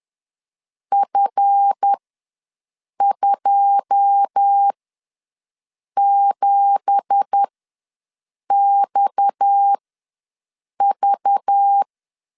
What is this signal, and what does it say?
Morse code, heard as a keyed tone, about 10 wpm, spaced as for 6 wpm, 800 Hz. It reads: F27XV
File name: LEZIONI / 9 2867 ws 10wpm 4215